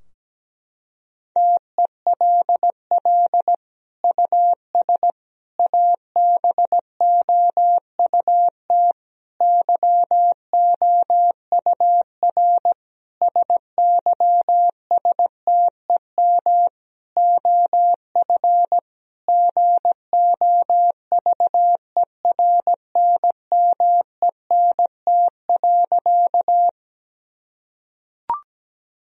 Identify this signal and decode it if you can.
Morse code, heard as a keyed tone, about 17 wpm, 700 Hz. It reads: TELL US ABOUT YOUR SYSTEM OF GOVERNMENT.
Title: Sentences from Top 1000 Words - Review Part 3 - 17wpm